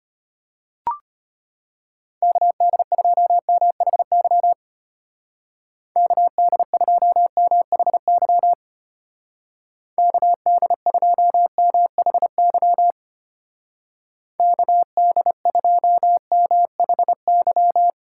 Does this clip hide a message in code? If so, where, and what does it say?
Yes: Morse code from 2.2 to 17.9 s, heard as a keyed tone, about 32 wpm, 700 Hz. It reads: KD2MHY KD2MHY KD2MHY KD2MHY